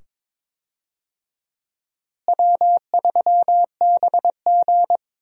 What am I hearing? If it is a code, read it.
Morse code, heard as a keyed tone, about 22 wpm, 700 Hz. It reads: W3BG
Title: USA Call Signs - 4 Characters - Repeated 2x - 22wpm